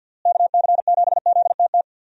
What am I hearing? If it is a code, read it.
Morse code, heard as a keyed tone, about 50 wpm, 700 Hz. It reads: KC7ZTT
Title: USA Call Signs - 6 Characters - Repeated 3x - 50wpm